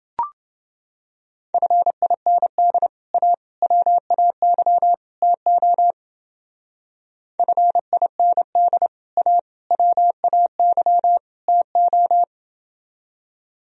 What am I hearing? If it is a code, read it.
Morse code, heard as a keyed tone, about 29 wpm, 700 Hz. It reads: FIND A WAY TO FIND A WAY TO